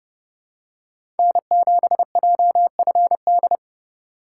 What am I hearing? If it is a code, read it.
Morse code, heard as a keyed tone, about 32 wpm, 700 Hz. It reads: N7JFD
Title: USA Call Signs - 5 Characters - 2x Character Spacing - 30wpm